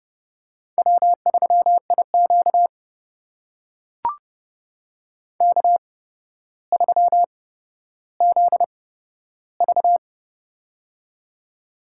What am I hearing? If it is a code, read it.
Morse code, heard as a keyed tone, about 31 wpm, spaced as for 13 wpm, 700 Hz. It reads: W3IQ K3ZV
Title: USA Call Signs - 4 Characters - 8x Character Spacing - 30wpm